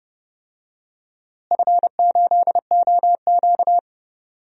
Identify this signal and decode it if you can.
Morse code, heard as a keyed tone, about 31 wpm, 700 Hz. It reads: F8OQ
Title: DXCC-100 Call Signs - 4 Characters - Repeated 3x - 30wpm